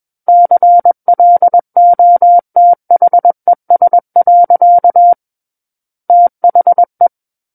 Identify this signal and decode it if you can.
Morse code, heard as a keyed tone, about 21 wpm, 700 Hz. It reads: CLOTHES. THE
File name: Book One / 12-Chapter 12 - 21 WPM